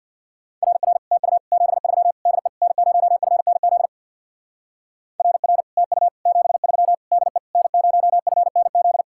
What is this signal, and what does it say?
Morse code, heard as a keyed tone, about 67 wpm, 700 Hz. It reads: RR TU 73 DE N0FNZ RR TU 73 DE N0FNZ